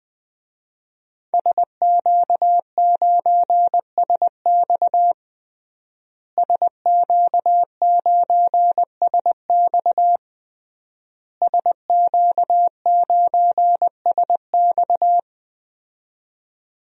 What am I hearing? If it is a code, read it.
Morse code, heard as a keyed tone, about 20 wpm, 700 Hz. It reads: SQ9SX SQ9SX SQ9SX
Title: DXCC-100 Call Signs - 5 Characters - Repeated 3x - 20wpm